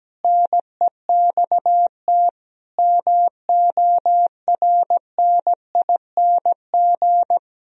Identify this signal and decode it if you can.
Morse code, heard as a keyed tone, about 17 wpm, 700 Hz. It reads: NEXT MORNING
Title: Sentences from Top 1000 Words - Review Part 1 - 17wpm